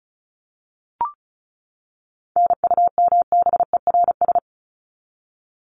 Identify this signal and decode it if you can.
Morse code, heard as a keyed tone, about 34 wpm, 700 Hz. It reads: NUMBERS
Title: ES:S Suffix - Top 100 Words with Base Words 35wpm